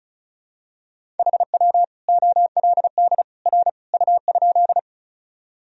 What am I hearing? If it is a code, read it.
Morse code, heard as a keyed tone, about 36 wpm, 700 Hz. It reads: HW OLD R U?
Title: Sentences from Top 200 Words and Abbreviations 35wpm